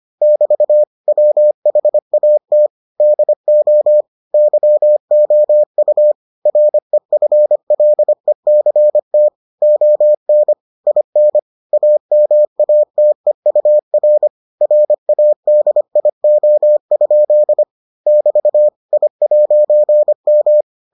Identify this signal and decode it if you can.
Morse code, heard as a keyed tone, about 25 wpm, 600 Hz. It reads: <BT> WHAT DO YOU REFLECT ON IN AMATEUR RADIO? <BT> I'M